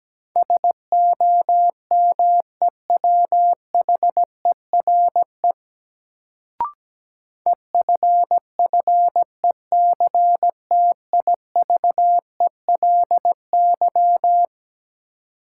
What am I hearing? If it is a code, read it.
Morse code, heard as a keyed tone, about 17 wpm, 700 Hz. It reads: SOMEWHERE EFFECTIVELY
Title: Sentences from Top 1400 Words - New Word Drill - 17wpm